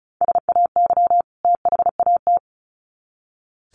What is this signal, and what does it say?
Morse code, heard as a keyed tone, about 34 wpm, 700 Hz. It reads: SAY THAT
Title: Sets of 2 Words 35wpm